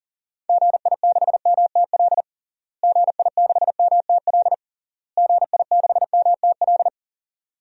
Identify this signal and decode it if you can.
Morse code, heard as a keyed tone, about 40 wpm, 700 Hz. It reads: GI6MTL GI6MTL GI6MTL
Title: DXCC-100 Call Signs - 6 Characters - Repeated 3x - 40wpm